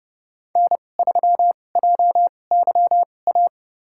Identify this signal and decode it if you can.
Morse code, heard as a keyed tone, about 30 wpm, spaced as for 22 wpm, 700 Hz. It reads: N3JYA
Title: USA Call Signs - 2x Character Spacing - 30wpm